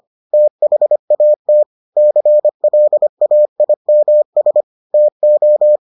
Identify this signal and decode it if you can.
Morse code, heard as a keyed tone, about 25 wpm, 600 Hz. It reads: THAT CLAIMS TO